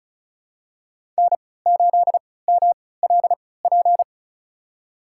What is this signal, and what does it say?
Morse code, heard as a keyed tone, about 36 wpm, spaced as for 20 wpm, 700 Hz. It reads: N8MLP
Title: USA Call Signs - 5 Characters - 3x Character Spacing - 35wpm